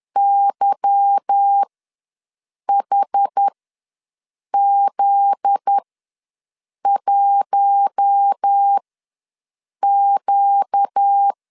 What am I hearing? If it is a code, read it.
Morse code, heard as a keyed tone, about 10 wpm, spaced as for 6 wpm, 800 Hz. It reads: YHZ1Q
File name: LEZIONI / 9 2 ws 10wpm 4130